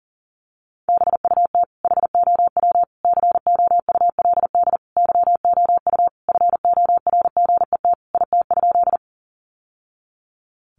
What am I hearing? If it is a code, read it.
Morse code, heard as a keyed tone, about 40 wpm, 700 Hz. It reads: BUT HOW COULD YOU FORGET IT?
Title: Sentences from Top 1200 Words 40wpm